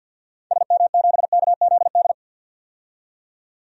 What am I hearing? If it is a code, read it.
Morse code, heard as a keyed tone, about 51 wpm, 700 Hz. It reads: SM7KZD
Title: DXCC-100 Call Signs - 6 Characters - 50wpm